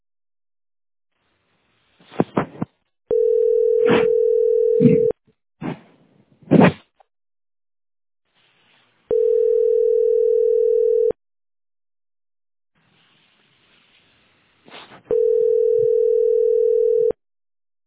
Ringback tone (North America), 440 + 480 Hz.